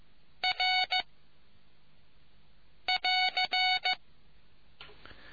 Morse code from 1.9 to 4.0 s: <AR>